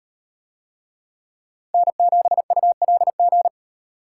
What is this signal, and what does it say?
Morse code, heard as a keyed tone, about 38 wpm, 700 Hz. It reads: N7ULG